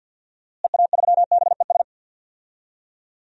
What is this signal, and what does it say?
Morse code, heard as a keyed tone, about 52 wpm, 700 Hz. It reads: EA3BES
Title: DXCC-100 Call Signs - 6 Characters - Repeated 3x - 50wpm